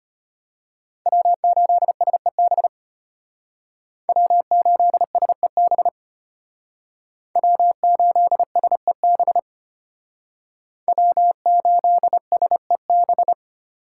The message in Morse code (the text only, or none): W8SEB W8SEB W8SEB W8SEB